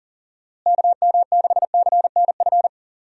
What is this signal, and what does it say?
Morse code, heard as a keyed tone, about 39 wpm, 700 Hz. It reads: KM6CNF
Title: USA Call Signs - 6 Characters - Repeated 2x - 40wpm